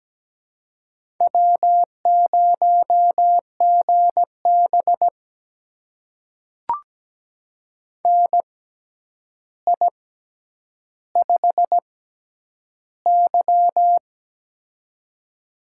Morse code, heard as a keyed tone, about 17 wpm, spaced as for 9 wpm, 700 Hz. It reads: W0GB NI5Y
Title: USA Call Signs - 4 Characters - 6x Character Spacing - 17wpm